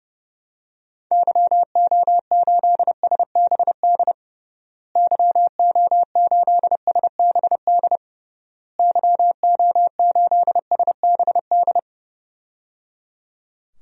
Morse code, heard as a keyed tone, about 30 wpm, 700 Hz. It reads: YO8SBD YO8SBD YO8SBD
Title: DXCC-100 Call Signs - 6 Characters - Repeated 3x - 30wpm